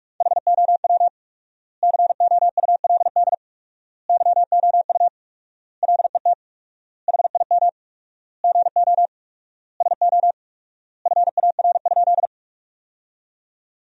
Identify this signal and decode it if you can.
Morse code, heard as a keyed tone, about 46 wpm, 700 Hz. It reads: HOW COULD YOU LET HIM GO SO FAR?
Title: 4x Word-Spacing - Sentences from Top 300 Words - 45wpm